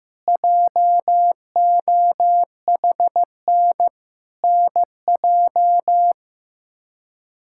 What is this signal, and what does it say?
Morse code, heard as a keyed tone, about 16 wpm, 700 Hz. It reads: JOHN NJ
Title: K1USN SST QSO Practice - 15wpm